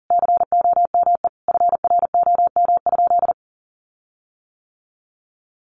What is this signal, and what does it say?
Morse code, heard as a keyed tone, about 40 wpm, 700 Hz. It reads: COME FROM?
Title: Sentences from Top 1500 Words 40wpm